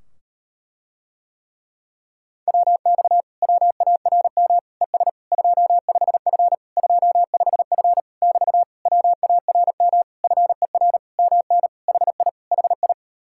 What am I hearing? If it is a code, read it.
Morse code, heard as a keyed tone, about 38 wpm, 700 Hz. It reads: WX WARM ES 25F 25F <BT> WARM FER MN HI HI